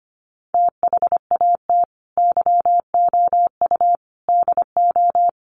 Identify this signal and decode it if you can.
Morse code, heard as a keyed tone, about 25 wpm, 700 Hz. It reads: THAT YOU DO
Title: Sets of 5 Words Encore 25wpm